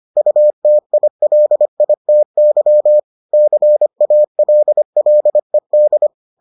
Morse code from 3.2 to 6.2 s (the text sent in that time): CALLED